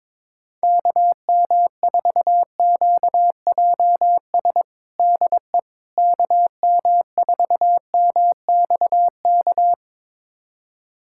Morse code, heard as a keyed tone, about 22 wpm, 700 Hz. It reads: KM4QJS DE KM4MXK
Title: USA Callsign DE Callsign - 6 Characters - 22wpm